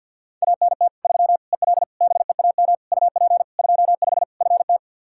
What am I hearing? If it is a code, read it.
Morse code, heard as a keyed tone, about 50 wpm, 700 Hz. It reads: ANT 3 EL BEAM UP 25 FT
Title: Practice QSO - Exchange 2 - Info - 50wpm